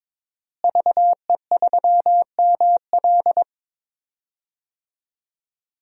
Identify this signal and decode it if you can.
Morse code, heard as a keyed tone, about 22 wpm, 700 Hz. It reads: VE3ML
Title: DXCC-100 Call Signs - 5 Characters - Repeated 3x - 22wpm